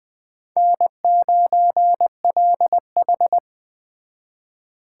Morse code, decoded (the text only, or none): N9LH